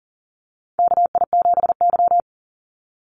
Morse code, heard as a keyed tone, about 40 wpm, 700 Hz. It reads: KI7Y
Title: Call Signs Repeated 2x - 40wpm